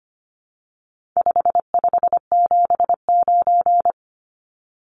Morse code, heard as a keyed tone, about 25 wpm, 700 Hz. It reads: 5579
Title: Numbers - 4-Digit 25wpm